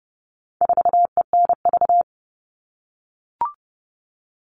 Morse code, heard as a keyed tone, about 30 wpm, 700 Hz. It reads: 4ENV
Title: All Cut Numbers - 4-Digit 30wpm